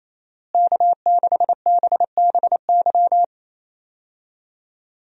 Morse code, heard as a keyed tone, about 28 wpm, 700 Hz. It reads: K6BBY
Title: USA Call Signs - 5 Characters - Repeated 2x - 28wpm